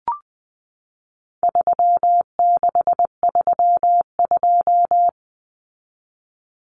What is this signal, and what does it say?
Morse code, heard as a keyed tone, about 20 wpm, 700 Hz. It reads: 3632
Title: Common Cut Numbers - 4-Digit 20wpm